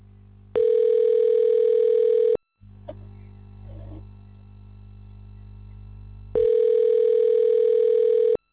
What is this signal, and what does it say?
Ringback tone (North America), 440 + 480 Hz.